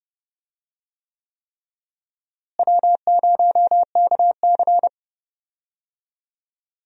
Morse code, heard as a keyed tone, about 30 wpm, 700 Hz. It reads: W0KC
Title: USA Call Signs - 4 Characters - Repeated 3x - 30wpm